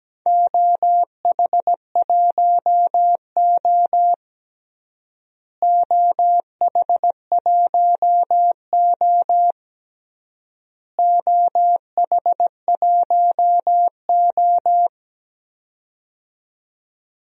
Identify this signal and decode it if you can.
Morse code, heard as a keyed tone, about 17 wpm, 700 Hz. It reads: OH1O OH1O OH1O
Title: DXCC-100 Call Signs - 4 Characters - Repeated 3x - 17wpm